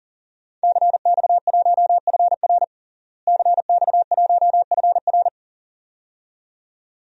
Morse code, heard as a keyed tone, about 40 wpm, 700 Hz. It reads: CX1FR CX1FR
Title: DXCC-100 Call Signs - 5 Characters - Repeated 2x - 40wpm